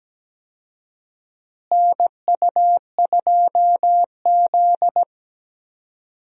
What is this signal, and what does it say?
Morse code, heard as a keyed tone, about 17 wpm, 700 Hz. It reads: NU2Z
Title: USA Call Signs - 4 Characters - Repeated 3x - 17wpm